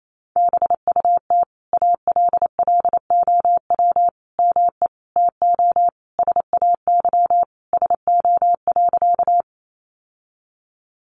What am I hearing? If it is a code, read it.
Morse code, heard as a keyed tone, about 28 wpm, 700 Hz. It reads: BUT ALLOW ME TO SAY SO.
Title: Sentences from Top 700 Words 28wpm